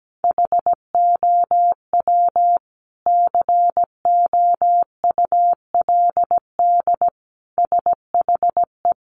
Morse code, heard as a keyed tone, about 17 wpm, 700 Hz. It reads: HOW COULD SHE
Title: Sentences from Top 1200 Words 17wpm